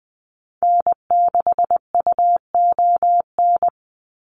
Morse code, heard as a keyed tone, about 20 wpm, 700 Hz. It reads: N6UON
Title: USA Call Signs 20wpm